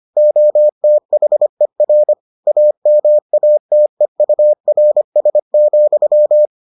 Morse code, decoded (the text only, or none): OTHER AMATEURS,